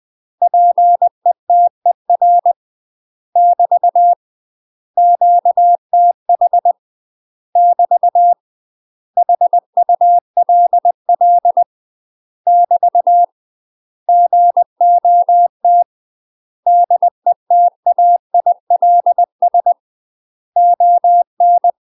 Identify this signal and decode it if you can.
Morse code, heard as a keyed tone, about 20 wpm, 700 Hz. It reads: PETER <BT> QTH <BT> HULL <BT> GOT DETAILS ON
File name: UK MASTER QSOs TRAINING / 20WPM / UK MASTER QSOs 0070